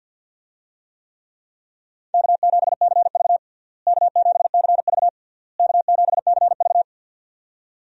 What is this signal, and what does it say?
Morse code, heard as a keyed tone, about 51 wpm, 700 Hz. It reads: K7CV K7CV K7CV
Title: USA Call Signs - 4 Characters - Repeated 3x - 50wpm